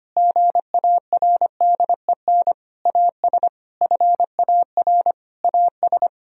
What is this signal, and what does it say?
Morse code, heard as a keyed tone, about 25 wpm, 700 Hz. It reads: GARDEN AS FAR AS
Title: Sentences from Top 1000 Words - Review Part 4 - 25wpm